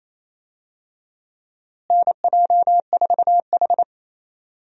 Morse code, heard as a keyed tone, about 28 wpm, 700 Hz. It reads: NJ4H
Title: USA Call Signs - 4 Characters - Repeated 2x - 28wpm